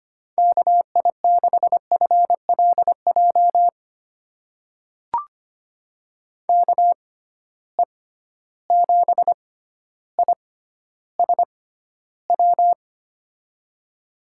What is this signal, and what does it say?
Morse code, heard as a keyed tone, about 25 wpm, spaced as for 13 wpm, 700 Hz. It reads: KI6FLJ KE7ISW
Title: USA Call Signs - 6x Character Spacing - 25wpm